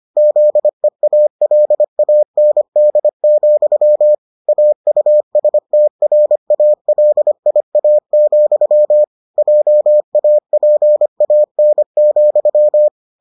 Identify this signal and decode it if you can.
Morse code, heard as a keyed tone, about 25 wpm, 600 Hz. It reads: ZEALAND, AUSTRALIA, JAPAN,